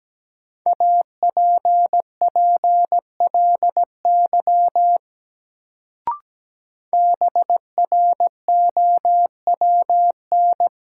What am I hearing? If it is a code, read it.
Morse code, heard as a keyed tone, about 17 wpm, 700 Hz. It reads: APPLY BROWN
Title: Sentences from Top 1200 Words - New Word Drill Alt - 17wpm